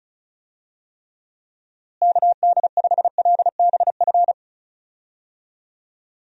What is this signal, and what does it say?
Morse code, heard as a keyed tone, about 36 wpm, 700 Hz. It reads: KD5LBF
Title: USA Call Signs - 6 Characters - Repeated 3x - 35wpm